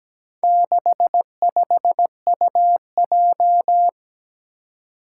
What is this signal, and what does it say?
Morse code, heard as a keyed tone, about 17 wpm, 700 Hz. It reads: 65UJ
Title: Four Challenging Letter-Number Combinations 17wpm